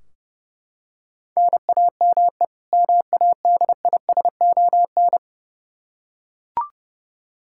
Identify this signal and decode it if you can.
Morse code, heard as a keyed tone, about 30 wpm, 700 Hz. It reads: NAME MADISON
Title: Practice QSO - Exchange 1 - Beginning - 30wpm